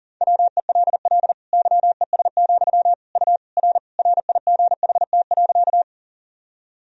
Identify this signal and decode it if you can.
Morse code, heard as a keyed tone, about 40 wpm, 700 Hz. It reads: WELL YES, U R RIGHT.
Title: Sentences from Top 500 Words and Abbreviations 40wpm